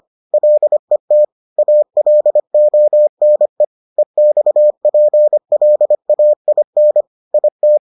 Morse code, heard as a keyed tone, about 25 wpm, 600 Hz. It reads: LET ALONE EXPLAIN IT